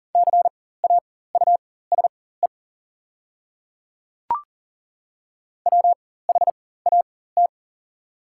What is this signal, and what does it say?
Morse code, heard as a keyed tone, about 40 wpm, spaced as for 19 wpm, 700 Hz. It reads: CAUSE WHAT
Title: 4x Character Spacing - Top 300 Words - 40wpm